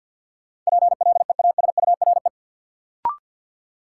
Morse code, heard as a keyed tone, about 52 wpm, 700 Hz. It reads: PLEASURE
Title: Sentences from Top 1600 Words - New Word Drill Alt - 50wpm